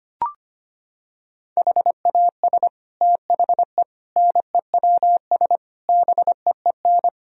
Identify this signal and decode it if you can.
Morse code, heard as a keyed tone, about 25 wpm, 700 Hz. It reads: HAS THE NEWS BEEN
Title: Sentences from Top 1000 Words - Review Part 3 - 25wpm